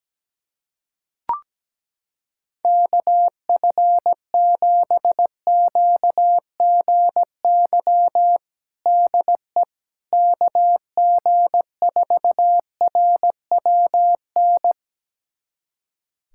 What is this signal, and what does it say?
Morse code, heard as a keyed tone, about 17 wpm, 700 Hz. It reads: KF7QGY DE KG4RWN
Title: USA Callsign DE Callsign - 6 Characters - 17wpm